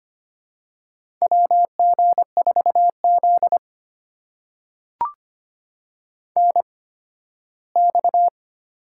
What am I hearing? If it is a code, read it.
Morse code, heard as a keyed tone, about 25 wpm, spaced as for 15 wpm, 700 Hz. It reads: WG4Z NX
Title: USA Call Signs - 4 Characters - 8x Character Spacing - 25wpm